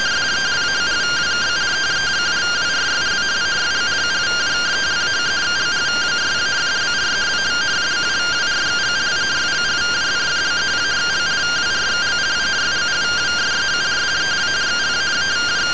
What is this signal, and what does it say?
Two-tone teleprinter signal, about 170 Hz shift.